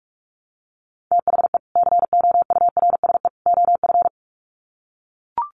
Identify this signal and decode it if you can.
Morse code, heard as a keyed tone, about 44 wpm, 700 Hz. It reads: THE COURSE OF
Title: Sets of 3 Words 45wpm